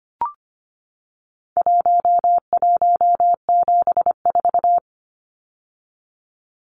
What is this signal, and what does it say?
Morse code, heard as a keyed tone, about 25 wpm, 700 Hz. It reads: 1174
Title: Numbers - Contest Style - 4-Digit 25wpm